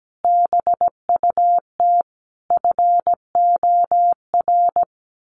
Morse code, heard as a keyed tone, about 17 wpm, 700 Hz. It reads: BUT FOR
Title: Sentences from Top 1400 Words 17wpm